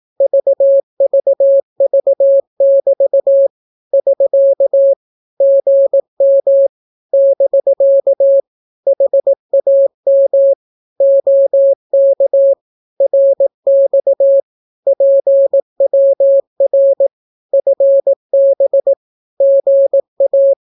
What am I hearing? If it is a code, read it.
Morse code, heard as a keyed tone, about 18 wpm, 550 Hz. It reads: VVV<BT> <SK> GM <BK> HAM OK RX PWR FB GA
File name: naukapodstaw / skroty / skroty